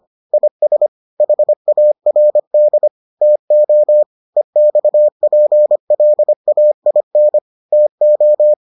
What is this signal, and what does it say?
Morse code, heard as a keyed tone, about 25 wpm, 600 Hz. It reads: IS HARD TO EXPLAIN TO